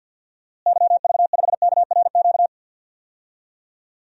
Morse code, heard as a keyed tone, about 46 wpm, 700 Hz. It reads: YV5KRQ
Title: DXCC-100 Call Signs - 6 Characters - Repeated 2x - 50wpm